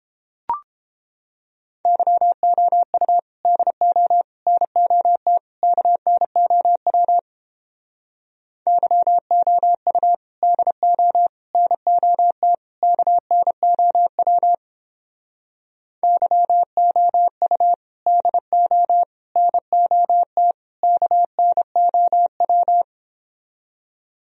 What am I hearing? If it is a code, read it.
Morse code, heard as a keyed tone, about 31 wpm, 700 Hz. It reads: YOU DO NOT KNOW YOU DO NOT KNOW YOU DO NOT KNOW